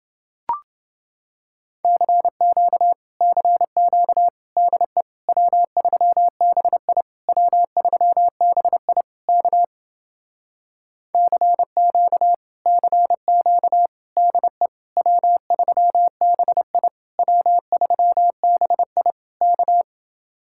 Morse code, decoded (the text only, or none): CQ CQ DE W3BI W3BI K CQ CQ DE W3BI W3BI K